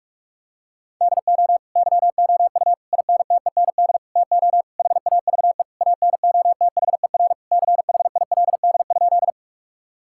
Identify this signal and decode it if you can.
Morse code, heard as a keyed tone, about 46 wpm, 700 Hz. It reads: DO YOU INTEND TO HAVE ANOTHER CHILD?
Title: Sentences from Top 1600 Words 45wpm